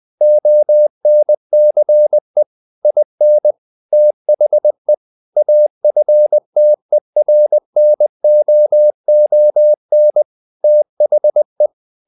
Morse code, heard as a keyed tone, about 20 wpm, 600 Hz. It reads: ONCE IN THE AFTERNOON THE
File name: the old man and the sea / the old man and the sea part 1 20wpm